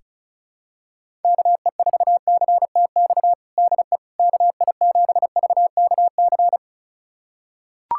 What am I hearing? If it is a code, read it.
Morse code, heard as a keyed tone, about 34 wpm, 700 Hz. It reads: KE4CTX DE KI7VKC
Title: USA Callsign DE Callsign - 6 Characters - 35wpm